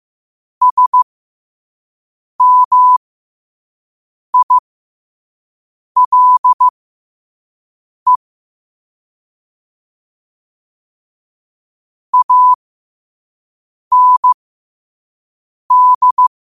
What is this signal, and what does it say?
Morse code, heard as a keyed tone, about 15 wpm, spaced as for 5 wpm, 1000 Hz. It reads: SMILE AND
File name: qotdfiles / 5wpm